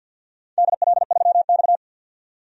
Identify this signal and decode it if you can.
Morse code, heard as a keyed tone, about 52 wpm, 700 Hz. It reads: DL3X
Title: DXCC-100 Call Signs - 4 Characters - 50wpm